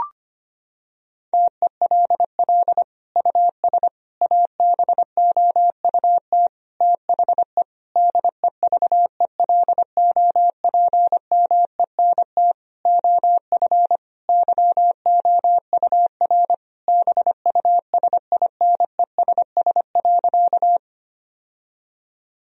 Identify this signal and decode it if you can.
Morse code, heard as a keyed tone, about 25 wpm, 700 Hz. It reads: TELL US ABOUT THE DEVELOPMENT OF YOUR BUSINESS.
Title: Sentences from Top 1000 Words - Review Part 4 - 25wpm